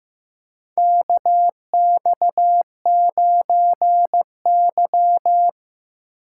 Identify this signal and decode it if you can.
Morse code, heard as a keyed tone, about 15 wpm, 700 Hz. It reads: KX9Y
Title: USA Call Signs - 4 Characters - 3x Character Spacing - 15wpm